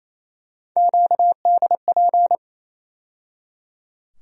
Morse code, heard as a keyed tone, about 28 wpm, 700 Hz. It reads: QDP
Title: USA Call Sign Postfixes - 3 Letters 28wpm